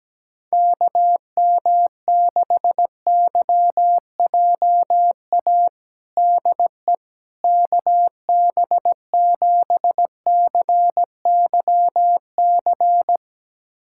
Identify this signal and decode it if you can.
Morse code, heard as a keyed tone, about 17 wpm, 700 Hz. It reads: KM6YJA DE KB7CYC
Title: USA Callsign DE Callsign - 6 Characters - 17wpm